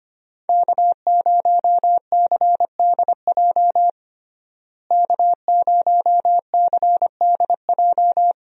Morse code, heard as a keyed tone, about 25 wpm, 700 Hz. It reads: K0CDJ K0CDJ
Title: USA Call Signs - 5 Characters - Repeated 2x - 25wpm